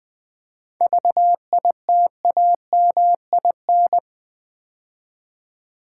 Morse code, heard as a keyed tone, about 21 wpm, 700 Hz. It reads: VITAMIN
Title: 7 Letter Words - First 3 Letters Given - 20wpm